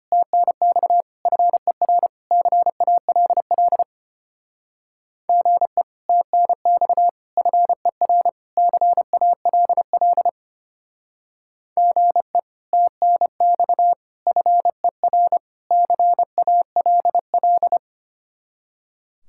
Morse code, heard as a keyed tone, about 32 wpm, 700 Hz. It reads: TNX FER CALL GE TNX FER CALL GE TNX FER CALL